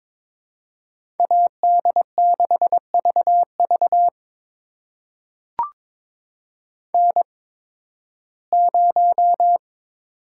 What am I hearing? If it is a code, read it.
Morse code, heard as a keyed tone, about 22 wpm, spaced as for 14 wpm, 700 Hz. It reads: AD6VV N0
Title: USA Call Signs - 5 Characters - 8x Character Spacing - 22wpm